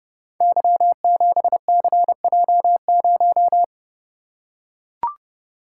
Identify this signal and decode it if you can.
Morse code, heard as a keyed tone, about 30 wpm, 700 Hz. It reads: Y7CJ0
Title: Code Groups - 5 Letter:Numbers - 30wpm